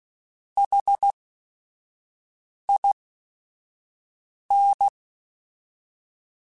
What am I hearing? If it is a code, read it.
Morse code, heard as a keyed tone, about 16 wpm, spaced as for 5 wpm, 800 Hz. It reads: HIN
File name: Technical Subjects / Technical Subjects-mp3s / OR Gate @5